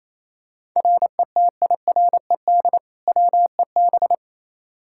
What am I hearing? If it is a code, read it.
Morse code, heard as a keyed tone, about 28 wpm, 700 Hz. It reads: RETIRED WEB
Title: Practice QSO - Exchange 3 - More Info - 28wpm